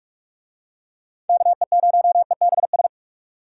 Morse code, heard as a keyed tone, about 41 wpm, 700 Hz. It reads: KE0EBS